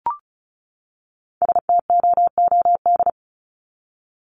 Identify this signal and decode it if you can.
Morse code, heard as a keyed tone, about 34 wpm, 700 Hz. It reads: STOOD
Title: Top 500 Words 35wpm